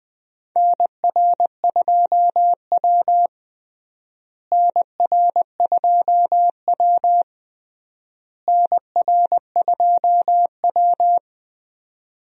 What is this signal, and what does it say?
Morse code, heard as a keyed tone, about 20 wpm, 700 Hz. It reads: NR2W NR2W NR2W
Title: USA Call Signs - 4 Characters - Repeated 3x - 20wpm